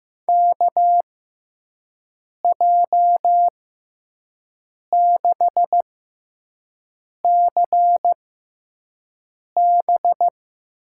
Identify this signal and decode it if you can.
Morse code, heard as a keyed tone, about 15 wpm, spaced as for 5 wpm, 700 Hz. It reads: KJ6CB
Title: USA Call Signs - 6 Characters - 6x Character Spacing - 15wpm